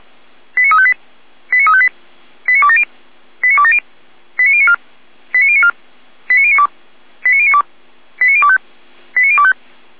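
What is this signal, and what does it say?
Five-tone selective call (ZVEI), heard as quick runs of five tones, tones of about 70 ms. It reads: 89358 89358 89270 89270 89094 89094 89092 89092 89025 89025
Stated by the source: ZVEI five-tone selective call received on SDR, code 89020.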